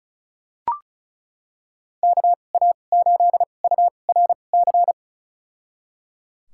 Morse code, heard as a keyed tone, about 34 wpm, spaced as for 25 wpm, 700 Hz. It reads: KA8URC
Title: USA Call Signs - 6 Characters - 2x Character Spacing - 35wpm